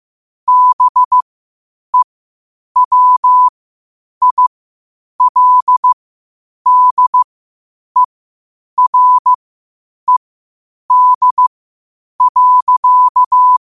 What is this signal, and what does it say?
Morse code, heard as a keyed tone, about 15 wpm, spaced as for 8 wpm, 1000 Hz. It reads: BEWILDERED.
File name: qotdfiles / 8wpm